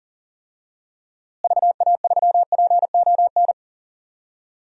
Morse code, heard as a keyed tone, about 40 wpm, 700 Hz. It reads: VA3PON
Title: DXCC-100 Call Signs - 6 Characters - Repeated 3x - 40wpm